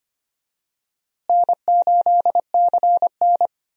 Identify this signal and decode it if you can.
Morse code, heard as a keyed tone, about 25 wpm, 700 Hz. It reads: N8CN
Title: USA Call Signs - 4 Characters - Repeated 2x - 25wpm